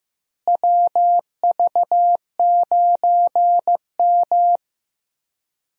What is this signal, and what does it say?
Morse code, heard as a keyed tone, about 15 wpm, 700 Hz. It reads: WV9M
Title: USA Call Signs - 4 Characters - Repeated 2x - 15wpm